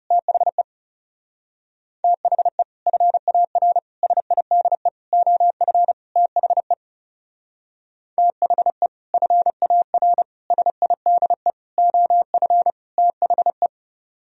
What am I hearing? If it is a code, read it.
Morse code, heard as a keyed tone, about 36 wpm, 700 Hz. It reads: THE THE FAR SIDE OF THE THE FAR SIDE OF THE
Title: Speed-Racing - Sets of 5 Words 30wpm